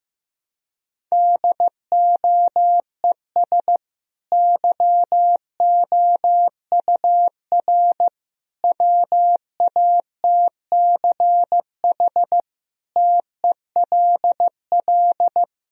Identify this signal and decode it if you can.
Morse code, heard as a keyed tone, about 15 wpm, 700 Hz. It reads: DOES YOUR WATCH TELL
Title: Sentences from Top 1000 Words - Review Part 1 - 15wpm